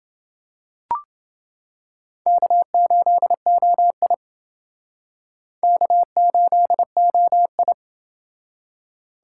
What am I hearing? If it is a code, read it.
Morse code, heard as a keyed tone, about 28 wpm, 700 Hz. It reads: K8OI K8OI